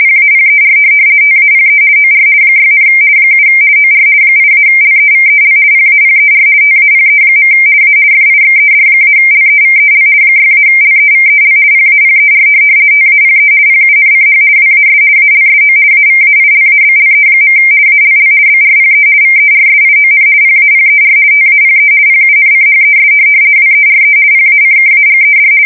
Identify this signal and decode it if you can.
RTTY, about 50 baud. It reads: PAPER TAPE OR 6, 7 OR 8-LEVEL PAPER TAPE. A FIXED-LEVEL 6-LEVEL PAPER TAPE PUNCH WAS ALSO AVAILABLE. ALL BRPE VERSION PAPER TAPE PUNCHES WERE PACKAGED
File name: fsk170-50B / teletype-corp